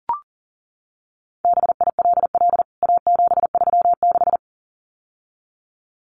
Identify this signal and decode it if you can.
Morse code, heard as a keyed tone, about 40 wpm, 700 Hz. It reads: BILL A736
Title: CWT Exchange 40wpm